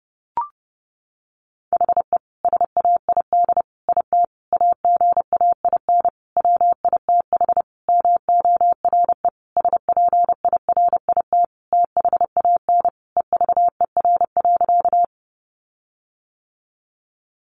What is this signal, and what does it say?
Morse code, heard as a keyed tone, about 30 wpm, 700 Hz. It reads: HE SAID IT AGAIN WITH MORE SPIRIT THAN EVER.
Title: Sentences from Top 1400 Words 30wpm